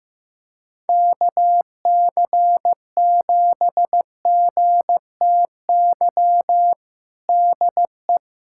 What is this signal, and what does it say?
Morse code, heard as a keyed tone, about 15 wpm, 700 Hz. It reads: KC7GTY DE
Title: USA Callsign DE Callsign - 6 Characters - 15wpm